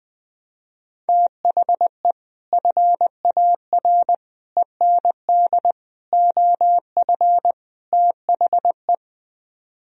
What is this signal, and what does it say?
Morse code, heard as a keyed tone, about 21 wpm, 700 Hz. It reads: THE FAR END OF THE
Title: Speed-Racing - Sets of 5 Words 20wpm